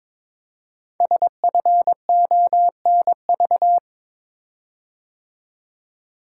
Morse code, heard as a keyed tone, about 22 wpm, 700 Hz. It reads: SFONV
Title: Code Groups - 5 Letters - 22wpm